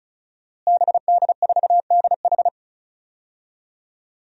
Morse code, heard as a keyed tone, about 36 wpm, 700 Hz. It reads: BD4DH
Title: DXCC-100 Call Signs - 5 Characters - Repeated 2x - 35wpm